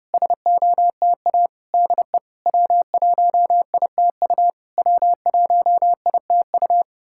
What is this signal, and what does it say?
Morse code, heard as a keyed tone, about 30 wpm, 690 Hz. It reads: SOTA DE W1ITU W1ITU
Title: SOTA QSO Practice 30wpm